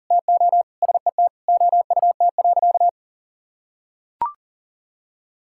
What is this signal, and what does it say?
Morse code, heard as a keyed tone, about 40 wpm, 700 Hz. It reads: TO SET OUT.
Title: Sentences from Top 1600 Words 40wpm